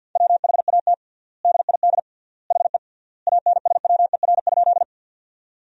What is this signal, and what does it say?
Morse code, heard as a keyed tone, about 50 wpm, 700 Hz. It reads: WHAT DID HE ANSWER?
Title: 3x Word-Spacing - Sentences from Top 300 Words - 50wpm